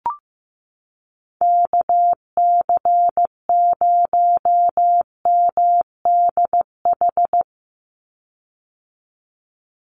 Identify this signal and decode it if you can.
Morse code, heard as a keyed tone, about 15 wpm, 700 Hz. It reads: KC0MDH
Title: USA Call Signs 15wpm